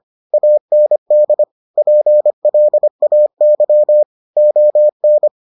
Morse code, heard as a keyed tone, about 25 wpm, 600 Hz. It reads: AND PLAY ON